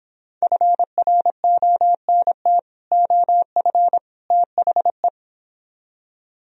Morse code, heard as a keyed tone, about 26 wpm, 700 Hz. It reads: FRONT OF THE